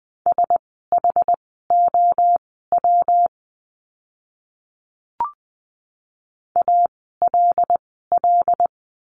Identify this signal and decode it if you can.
Morse code, heard as a keyed tone, about 20 wpm, spaced as for 15 wpm, 700 Hz. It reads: SHOW ALL
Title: 230 Lesson 37f - Letter x - Words 20wpm